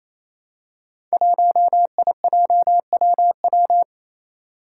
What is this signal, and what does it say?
Morse code, heard as a keyed tone, about 27 wpm, 700 Hz. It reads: 1IJWW
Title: Five Challenging Letter-Number Combinations 28wpm